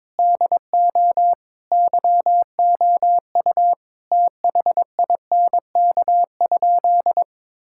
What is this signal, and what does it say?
Morse code, heard as a keyed tone, about 21 wpm, 700 Hz. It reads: DO YOU THINK?
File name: practice / 22wpm / Sentences with 2-Word Phrases - 22wpm